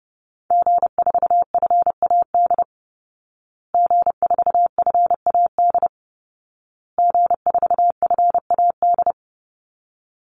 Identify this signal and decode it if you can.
Morse code, heard as a keyed tone, about 30 wpm, 700 Hz. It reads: G4FAD G4FAD G4FAD
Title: Call Signs Repeated 2x - 30wpm